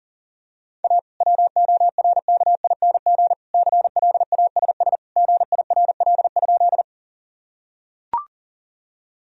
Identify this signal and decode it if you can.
Morse code, heard as a keyed tone, about 40 wpm, 700 Hz. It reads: A WORKING CLASS GIRL?
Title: Sentences from Top 1000 Words - Review Part 4 - 40wpm